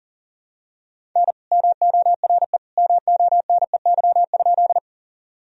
Morse code, heard as a keyed tone, about 40 wpm, 700 Hz. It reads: N MORE MONEY?
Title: Sentences from Top 400 Words and Abbreviations 40wpm